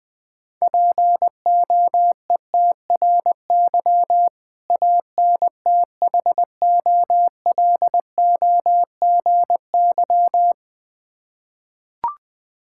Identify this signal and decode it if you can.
Morse code, heard as a keyed tone, about 20 wpm, 700 Hz. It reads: POETRY ANTHOLOGY
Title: Difficult Words with Hints - 20wpm